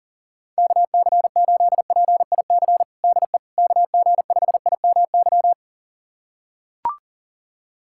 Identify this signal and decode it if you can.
Morse code, heard as a keyed tone, about 40 wpm, 700 Hz. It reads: KC8PIC DE KG5IMY
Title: USA Callsign DE Callsign - 6 Characters - 40wpm